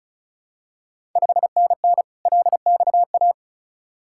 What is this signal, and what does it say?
Morse code, heard as a keyed tone, about 36 wpm, 700 Hz. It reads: 5NN LXA
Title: Ontario QSO Party Exchange 35wpm